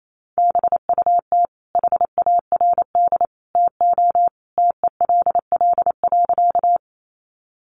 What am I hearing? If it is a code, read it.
Morse code, heard as a keyed tone, about 28 wpm, 700 Hz. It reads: BUT HARD TO TELL.
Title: Sentences from Top 300 Words 28wpm